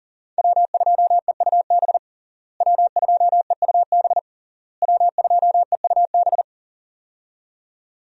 Morse code, heard as a keyed tone, about 37 wpm, 700 Hz. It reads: W2EUB W2EUB W2EUB
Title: USA Call Signs - 5 Characters - Repeated 3x - 40wpm